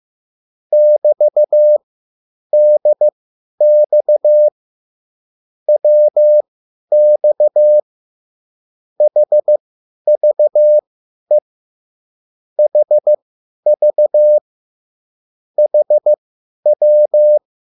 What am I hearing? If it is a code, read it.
Morse code, heard as a keyed tone, about 15 wpm, spaced as for 10 wpm, 600 Hz. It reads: <BT>DX WX HVE HV HW